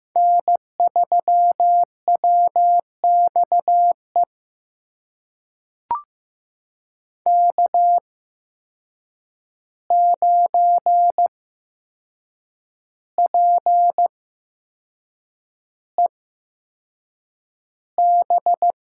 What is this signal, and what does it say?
Morse code, heard as a keyed tone, about 15 wpm, spaced as for 6 wpm, 700 Hz. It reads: N3WXE K9PEB
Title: USA Call Signs - 5 Characters - 8x Character Spacing - 15wpm